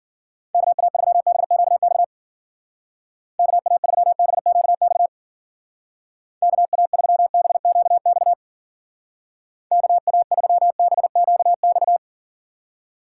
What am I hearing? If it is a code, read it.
Morse code, heard as a keyed tone, about 49 wpm, 700 Hz. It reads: KA3BQX KA3BQX KA3BQX KA3BQX
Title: Speed Racing - USA 2x3 Call Signs 40wpm